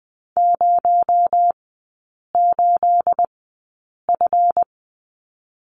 Morse code, heard as a keyed tone, about 20 wpm, spaced as for 8 wpm, 700 Hz. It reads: 08F
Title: 218 Lesson 35e - Number 0 - Three Characters 20wpm